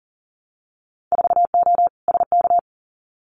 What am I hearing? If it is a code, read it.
Morse code, heard as a keyed tone, about 39 wpm, 700 Hz. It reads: 4O SK
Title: Winter Field Day Exchanges Encore 40wpm